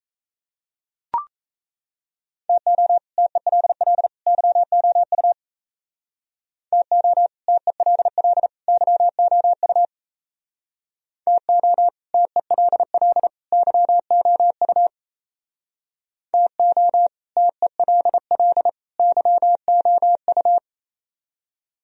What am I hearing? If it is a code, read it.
Morse code, heard as a keyed tone, about 35 wpm, 700 Hz. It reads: TO TELL YOU TO TELL YOU TO TELL YOU TO TELL YOU
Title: Speed-Racing - Sets of 3 Words 28wpm